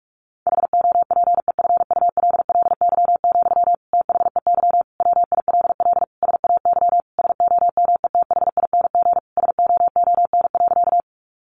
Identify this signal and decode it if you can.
Morse code, heard as a keyed tone, about 45 wpm, 700 Hz. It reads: HOPEFULLY, THEY WILL SAY SOMETHING SOON.
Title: Sentences from Top 1500 Words 45wpm